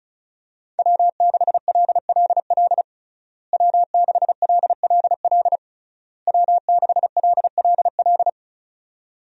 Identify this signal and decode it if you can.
Morse code, heard as a keyed tone, about 36 wpm, 700 Hz. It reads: W6LLL W6LLL W6LLL
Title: USA Call Signs - 5 Characters - Repeated 3x - 35wpm